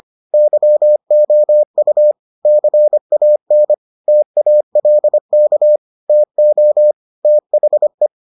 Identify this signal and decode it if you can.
Morse code, heard as a keyed tone, about 25 wpm, 600 Hz. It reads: YOU CAN TALK TO THE